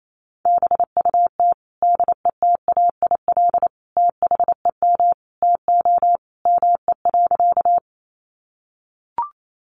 Morse code, heard as a keyed tone, about 28 wpm, 700 Hz. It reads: BUT DETAIL THEM TO ME.
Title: Sentences from Top 1500 Words 28wpm